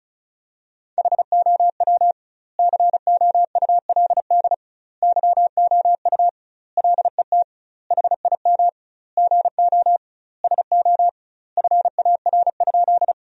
Morse code, heard as a keyed tone, about 36 wpm, 700 Hz. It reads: HOW COULD YOU LET HIM GO SO FAR?
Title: 2x Word-Spacing - Sentences from Top 300 Words - 35wpm